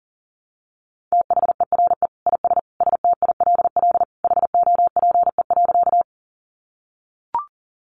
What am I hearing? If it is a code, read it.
Morse code, heard as a keyed tone, about 40 wpm, 700 Hz. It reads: THERE IS STILL HOPE.
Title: Sentences from Top 1000 Words 40wpm